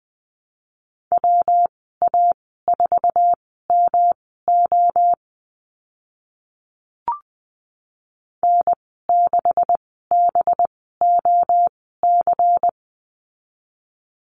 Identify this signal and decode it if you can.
Morse code, heard as a keyed tone, about 20 wpm, spaced as for 14 wpm, 700 Hz. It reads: WA4MO N6BOC
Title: 186 Lesson 30g - Letter b - Call Signs 20wpm